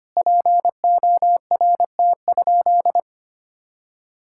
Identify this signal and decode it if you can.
Morse code, heard as a keyed tone, about 25 wpm, 700 Hz. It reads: PORT?
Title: Sentences from Top 1000 Words - Review Part 5 - 25wpm